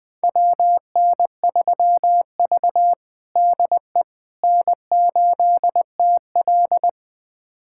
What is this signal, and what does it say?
Morse code, heard as a keyed tone, about 21 wpm, 700 Hz. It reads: WN3V DE N8TL
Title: USA Callsign DE Callsign - 4 Characters - 20wpm